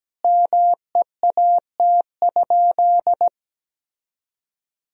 Morse code, heard as a keyed tone, about 17 wpm, 700 Hz. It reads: MEAT?
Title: Sentences from Top 1000 Words - Review Part 3 - 17wpm